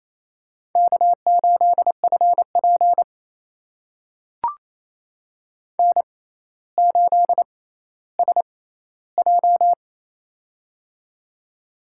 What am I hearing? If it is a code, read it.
Morse code, heard as a keyed tone, about 28 wpm, spaced as for 14 wpm, 700 Hz. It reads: K8FP N8SJ
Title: USA Call Signs - 4 Characters - 6x Character Spacing - 28wpm